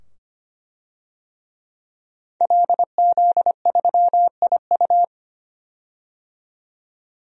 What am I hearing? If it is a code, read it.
Morse code, heard as a keyed tone, about 25 wpm, 700 Hz. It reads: LZ3IU
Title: DXCC-100 Call Signs - 5 Characters - 25wpm